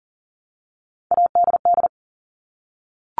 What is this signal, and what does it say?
Morse code, heard as a keyed tone, about 40 wpm, 690 Hz. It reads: ADD
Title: Multi-letter Phonemes Repeated 3x 40wpm